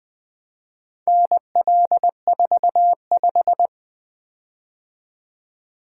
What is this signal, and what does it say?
Morse code, heard as a keyed tone, about 20 wpm, 700 Hz. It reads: NL45
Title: Four Letter-Number Combinations - 20wpm